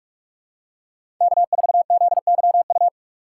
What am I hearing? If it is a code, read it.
Morse code, heard as a keyed tone, about 46 wpm, 700 Hz. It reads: K4ZYU
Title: USA Call Signs - 4x Character Spacing - 45wpm